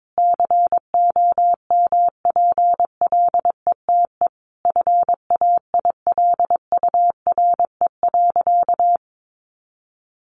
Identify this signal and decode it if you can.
Morse code, heard as a keyed tone, about 22 wpm, 700 Hz. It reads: COMPLETE FAILURE.
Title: Sentences from Top 1400 Words 22wpm